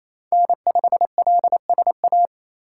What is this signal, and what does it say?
Morse code, heard as a keyed tone, about 27 wpm, 700 Hz. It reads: N5LSA
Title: USA Call Signs - 5 Characters - 4x Character Spacing - 28wpm